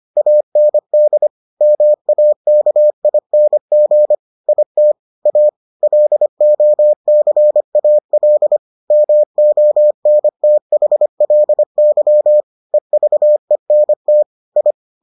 Morse code, heard as a keyed tone, about 25 wpm, 600 Hz. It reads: AND MAKING IT A LOCAL MONTHLY EVENT I